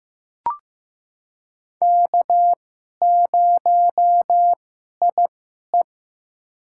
Morse code, heard as a keyed tone, about 15 wpm, spaced as for 11 wpm, 700 Hz. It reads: K0IE
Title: USA Call Signs - 2x Character Spacing - 15wpm